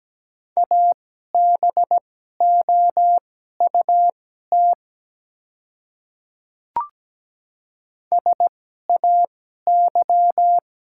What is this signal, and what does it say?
Morse code, heard as a keyed tone, about 17 wpm, spaced as for 12 wpm, 700 Hz. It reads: ABOUT SAY
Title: 2x Character Spacing - Top 300 Words - 17wpm